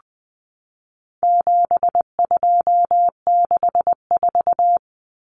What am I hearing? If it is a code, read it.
Morse code, heard as a keyed tone, about 20 wpm, 700 Hz. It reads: 7264
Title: Numbers - 4-Digit 20wpm